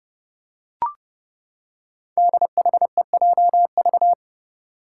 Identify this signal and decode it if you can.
Morse code, heard as a keyed tone, about 30 wpm, 700 Hz. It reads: DHEJV
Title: Five Challenging Letter-Number Combinations 30wpm